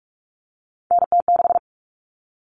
Morse code, heard as a keyed tone, about 44 wpm, 700 Hz. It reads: NT6
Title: All Cut Numbers - 3-Digit 45wpm